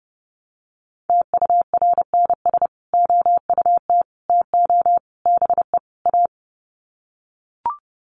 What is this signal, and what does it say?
Morse code, heard as a keyed tone, about 30 wpm, 700 Hz. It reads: TURNS OUT TO BE A
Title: Sets of 5 Words Encore 30wpm